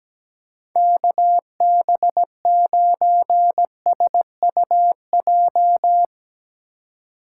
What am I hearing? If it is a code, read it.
Morse code, heard as a keyed tone, about 18 wpm, 700 Hz. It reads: KB9SUJ
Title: Speed Racing - USA 2x3 Call Signs 17wpm